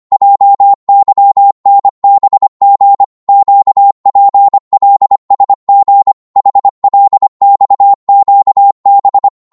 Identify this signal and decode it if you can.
Morse code, heard as a keyed tone, about 25 wpm, 800 Hz. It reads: JYNBG QPLSG HLXQB
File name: Morse / ltr-25wpm